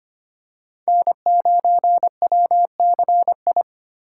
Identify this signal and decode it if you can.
Morse code, heard as a keyed tone, about 25 wpm, 700 Hz. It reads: N9WCI
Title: USA Call Signs - 5 Characters - 4x Character Spacing - 25wpm